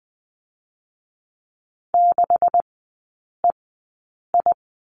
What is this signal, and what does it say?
Morse code, heard as a keyed tone, about 20 wpm, spaced as for 8 wpm, 700 Hz. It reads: 6EI
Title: 138 Lesson 23e - Letter f - Three Characters 20wpm